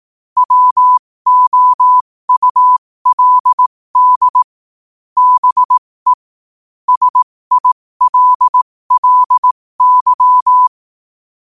Morse code, heard as a keyed tone, about 18 wpm, spaced as for 15 wpm, 1000 Hz. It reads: WOULD BE SILLY
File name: qotdfiles / 15wpm